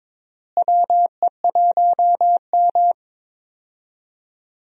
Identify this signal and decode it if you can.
Morse code, heard as a keyed tone, about 22 wpm, 700 Hz. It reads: WE1M
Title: USA Call Signs - 4 Characters - Repeated 3x - 22wpm